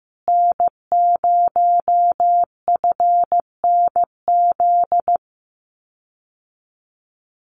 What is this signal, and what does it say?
Morse code, heard as a keyed tone, about 15 wpm, 700 Hz. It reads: N0FNZ
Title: Common QSO Phrases Encore 15wpm